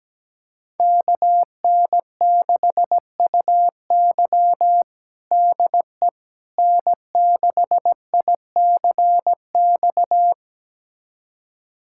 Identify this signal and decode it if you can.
Morse code, heard as a keyed tone, about 17 wpm, 700 Hz. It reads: KN6UY DE N6ICX
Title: USA Callsign DE Callsign - 5 Characters - 17wpm